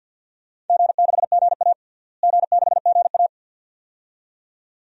Morse code, heard as a keyed tone, about 49 wpm, 700 Hz. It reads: G6GA G6GA
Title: DXCC-100 Call Signs - 4 Characters - Repeated 2x - 50wpm